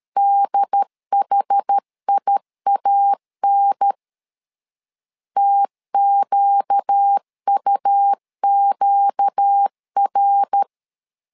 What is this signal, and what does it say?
Morse code, heard as a keyed tone, about 13 wpm, 800 Hz. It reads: DHIAN TQUQR
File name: LEZIONI / 4 Q 12wpm 1833